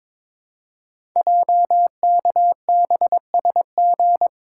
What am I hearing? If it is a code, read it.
Morse code, heard as a keyed tone, about 22 wpm, 700 Hz. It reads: JKBSG
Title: Code Groups - 5 Letters - 22wpm